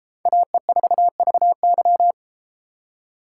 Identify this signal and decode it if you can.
Morse code, heard as a keyed tone, about 31 wpm, 700 Hz. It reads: AE4VY